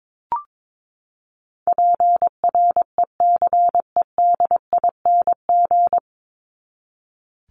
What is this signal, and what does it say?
Morse code, heard as a keyed tone, about 22 wpm, 700 Hz. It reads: PRECEDING
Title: ING Suffix - Top 100 Words with Base Words 22wpm